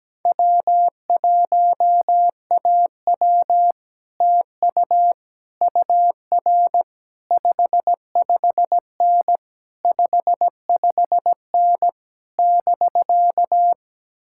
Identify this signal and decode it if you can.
Morse code, heard as a keyed tone, about 17 wpm, 700 Hz. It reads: W1AW TU UR 55N 55N <BK>
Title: Speed-Racing Common QSO Phrases - 17wpm